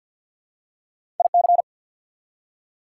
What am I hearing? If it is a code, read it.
Morse code, heard as a keyed tone, about 54 wpm, 700 Hz. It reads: IC
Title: Prefixes and Suffixes 50wpm